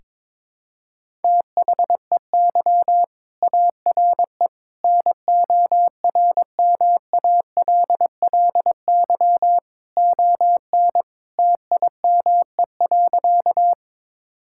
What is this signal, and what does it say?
Morse code, heard as a keyed tone, about 22 wpm, 700 Hz. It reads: THEY ARE NORMALLY ON TIME.
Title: Sentences from Top 1000 Words - Review Part 6 - 22wpm